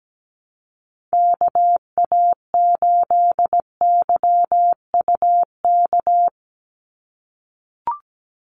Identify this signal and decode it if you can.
Morse code, heard as a keyed tone, about 17 wpm, 700 Hz. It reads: KA8YUK
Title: USA Call Signs 17wpm